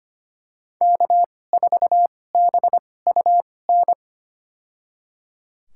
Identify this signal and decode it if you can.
Morse code, heard as a keyed tone, about 25 wpm, spaced as for 18 wpm, 700 Hz. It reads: K4BUN
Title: USA Call Signs - 5 Characters - 2x Character Spacing - 25wpm